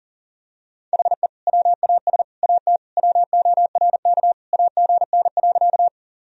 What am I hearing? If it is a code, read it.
Morse code, heard as a keyed tone, about 40 wpm, 700 Hz. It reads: HE WAS AT WORK AGN.
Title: Sentences from Top 200 Words and Abbreviations 40wpm